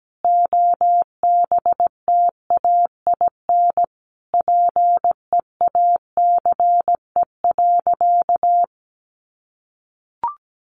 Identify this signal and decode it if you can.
Morse code, heard as a keyed tone, about 17 wpm, 700 Hz. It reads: OBTAIN PEACE.
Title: Sentences from Top 1500 Words 17wpm